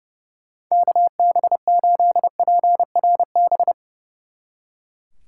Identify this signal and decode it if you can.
Morse code, heard as a keyed tone, about 38 wpm, 700 Hz. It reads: KB8PRB